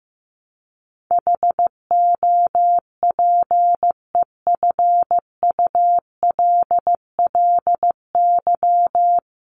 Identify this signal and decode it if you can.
Morse code, heard as a keyed tone, about 15 wpm, 700 Hz. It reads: HOPEFULLY
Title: Top 1251-1500 Words 15wpm